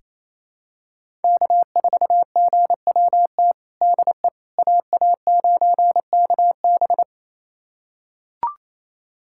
Morse code, heard as a keyed tone, about 28 wpm, 700 Hz. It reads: K4GWT DE AA9KB
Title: USA Callsign DE Callsign 28wpm